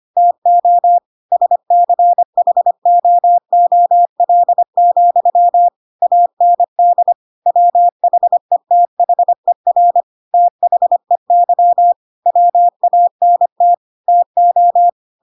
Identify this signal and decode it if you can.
Morse code, heard as a keyed tone, about 25 wpm, 700 Hz. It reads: TO SCHOOL, AND WHETHER THEY WANT TO
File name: Pinocchio / Pinocchio ch4 25 25 ik1hge